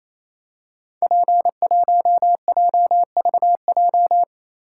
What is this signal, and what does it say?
Morse code, heard as a keyed tone, about 27 wpm, 700 Hz. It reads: P1JVJ
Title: Code Groups - 5 Letter:Numbers - 28wpm